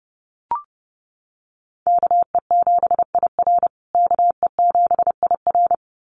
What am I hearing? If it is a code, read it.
Morse code, heard as a keyed tone, about 30 wpm, 700 Hz. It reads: KE7IR KE7IR
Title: USA Call Signs Repeated 2x 30wpm